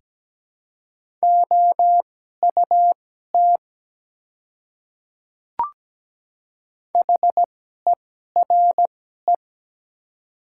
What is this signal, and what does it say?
Morse code, heard as a keyed tone, about 17 wpm, spaced as for 12 wpm, 700 Hz. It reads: OUT HERE
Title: 2x Character Spacing - Top 200 Words - 17wpm